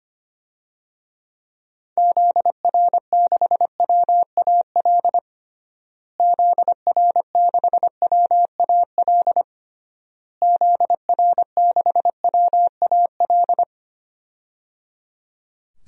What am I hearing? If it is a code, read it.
Morse code, heard as a keyed tone, about 25 wpm, 700 Hz. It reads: ZR6WAL ZR6WAL ZR6WAL
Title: DXCC-100 Call Signs - 6 Characters - Repeated 3x - 25wpm